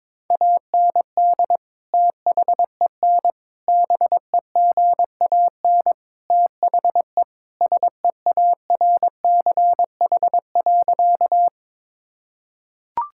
Morse code, heard as a keyed tone, about 22 wpm, 700 Hz. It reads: AND THEN BEGAN THE SEARCH.
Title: Sentences from Top 1000 Words - Review Part 5 - 22wpm